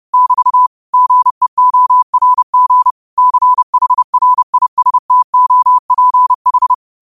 Morse code, heard as a keyed tone, about 30 wpm, 1000 Hz. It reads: <BT> GEORG CHRISTOPH
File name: qotdfiles / 30wpm